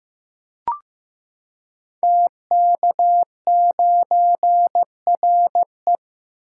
Morse code, heard as a keyed tone, about 15 wpm, 700 Hz. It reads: TK9RE
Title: Code Groups - 5 Letter:Numbers - 15wpm